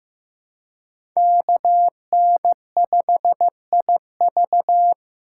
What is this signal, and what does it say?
Morse code, heard as a keyed tone, about 15 wpm, 700 Hz. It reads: KN5IV
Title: Code Groups - 5 Letter:Numbers - 15wpm